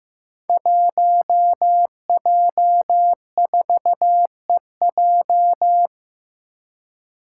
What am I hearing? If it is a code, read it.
Morse code, heard as a keyed tone, about 15 wpm, 700 Hz. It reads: 1J4EJ
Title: Five Challenging Letter-Number Combinations 15wpm